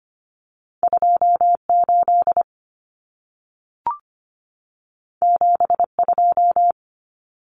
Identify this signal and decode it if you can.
Morse code, heard as a keyed tone, about 25 wpm, 700 Hz. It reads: 28 72
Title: Numbers - 2-Digit 25wpm